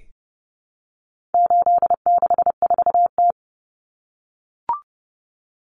Morse code, heard as a keyed tone, about 30 wpm, 700 Hz. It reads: 864T
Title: Common Cut Numbers - 4-Digit 30wpm